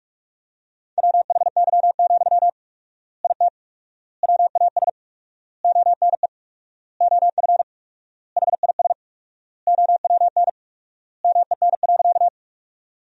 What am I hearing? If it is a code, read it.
Morse code, heard as a keyed tone, about 46 wpm, 700 Hz. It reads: WHY, IT WAS ONE OF HIS OWN MEN.
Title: 4x Word-Spacing - Sentences from Top 300 Words - 45wpm